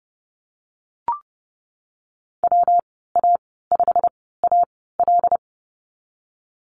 Morse code, heard as a keyed tone, about 30 wpm, spaced as for 17 wpm, 700 Hz. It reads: WA5AL
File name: Lesson 19g - Letter w - Call Signs 30wpm